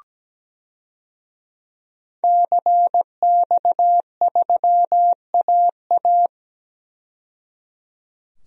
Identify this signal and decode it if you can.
Morse code, heard as a keyed tone, about 17 wpm, 700 Hz. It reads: CX3AA
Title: DXCC-100 Call Signs - 5 Characters - 17wpm